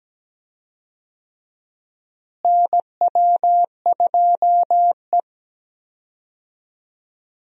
Morse code, heard as a keyed tone, about 17 wpm, 700 Hz. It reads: NW2E
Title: USA Call Signs - 4 Characters - Repeated 3x - 17wpm